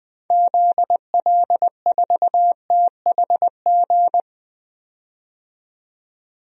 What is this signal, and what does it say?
Morse code, heard as a keyed tone, about 20 wpm, 700 Hz. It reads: ZL4THG
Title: DXCC-100 Call Signs - 6 Characters - 20wpm